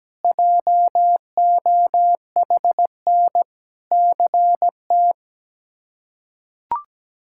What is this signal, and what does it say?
Morse code, heard as a keyed tone, about 17 wpm, 700 Hz. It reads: JOHN CT
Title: K1USN SST QSO Practice - 17wpm